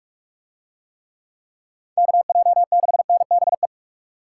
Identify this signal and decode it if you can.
Morse code, heard as a keyed tone, about 44 wpm, 700 Hz. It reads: KJ6NBE